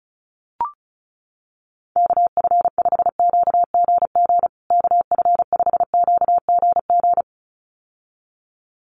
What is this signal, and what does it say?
Morse code, heard as a keyed tone, about 34 wpm, 700 Hz. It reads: KF5QGG KF5QGG
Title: USA Call Signs Repeated 2x 35wpm